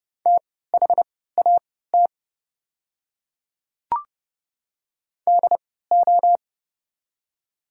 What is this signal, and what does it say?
Morse code, heard as a keyed tone, about 30 wpm, spaced as for 17 wpm, 700 Hz. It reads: THAT DO
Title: 3x Character Spacing - Top 100 Words - 30wpm